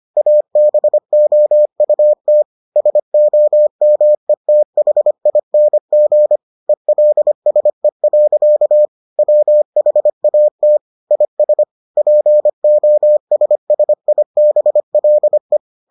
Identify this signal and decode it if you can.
Morse code, heard as a keyed tone, about 25 wpm, 600 Hz. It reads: ABOUT SOMETHING ELSE. WHAT IS POSSIBLE